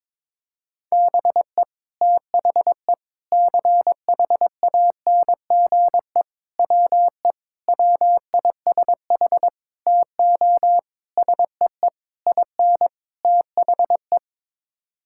Morse code, heard as a keyed tone, about 22 wpm, 700 Hz. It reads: BE THE CHANGE WE WISH TO SEE IN THE
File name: Sets of 9 Words - 22wpm